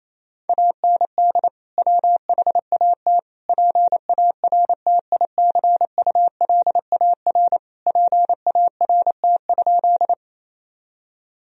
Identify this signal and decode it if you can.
Morse code, heard as a keyed tone, about 28 wpm, 700 Hz. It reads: AND WHAT PARTICULAR PART?
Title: Sentences from Top 1000 Words - Review Part 4 - 28wpm